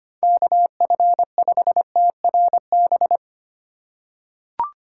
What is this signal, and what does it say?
Morse code, heard as a keyed tone, about 25 wpm, 700 Hz. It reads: KF5TRB
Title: USA Call Signs - 2x Character Spacing - 25wpm